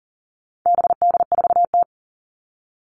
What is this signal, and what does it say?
Morse code, heard as a keyed tone, about 40 wpm, 700 Hz. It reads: BD4T
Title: All Cut Numbers - 4-Digit 40wpm